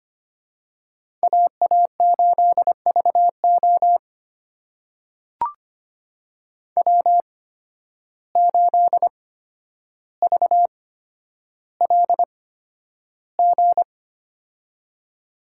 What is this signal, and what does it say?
Morse code, heard as a keyed tone, about 25 wpm, spaced as for 11 wpm, 700 Hz. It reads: AA8VO W8VLG
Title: USA Call Signs - 5 Characters - 8x Character Spacing - 25wpm